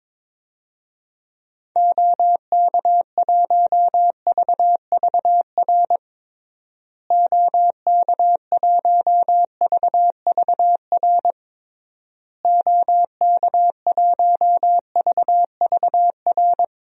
Morse code, heard as a keyed tone, about 22 wpm, 700 Hz. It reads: OK1VVR OK1VVR OK1VVR
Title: DXCC-100 Call Signs - 6 Characters - Repeated 3x - 22wpm